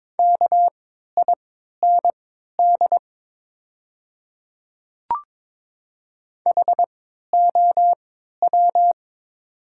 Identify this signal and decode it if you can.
Morse code, heard as a keyed tone, about 22 wpm, spaced as for 12 wpm, 700 Hz. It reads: KIND HOW
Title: 3x Character Spacing - Top 200 Words - 22wpm